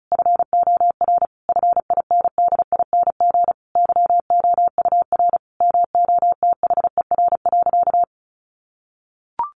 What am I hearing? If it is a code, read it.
Morse code, heard as a keyed tone, about 34 wpm, 700 Hz. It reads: FOR FINDING YOUR MOTHER.
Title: Sentences from Top 1300 Words 35wpm